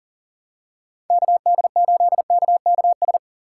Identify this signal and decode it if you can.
Morse code, heard as a keyed tone, about 40 wpm, 700 Hz. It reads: KD8KKS
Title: USA Call Signs - 6 Characters - 6x Character Spacing - 40wpm